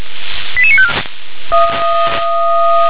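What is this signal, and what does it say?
Five-tone selective call (ZVEI), heard as quick runs of five tones, tones of about 70 ms. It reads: 88954
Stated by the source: ZVEI five-tone selective call received with an SDR.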